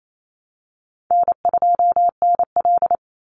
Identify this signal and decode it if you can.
Morse code, heard as a keyed tone, about 28 wpm, 700 Hz. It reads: N2NL
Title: Call Signs Repeated 2x - 28wpm